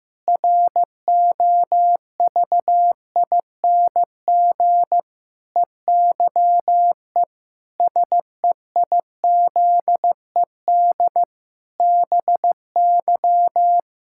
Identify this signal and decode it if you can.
Morse code, heard as a keyed tone, about 15 wpm, 700 Hz. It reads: ROVING EYE SEIZED BY
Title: Pangram Challenge v2 - 15wpm